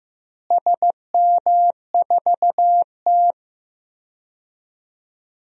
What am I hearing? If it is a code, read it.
Morse code, heard as a keyed tone, about 16 wpm, 700 Hz. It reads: SM4T
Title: DXCC-100 Call Signs - 4 Characters - Repeated 3x - 15wpm